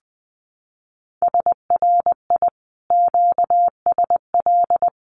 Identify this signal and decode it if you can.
Morse code, heard as a keyed tone, about 20 wpm, 700 Hz. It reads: SRI QSL
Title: Common QSO Phrases Encore 20wpm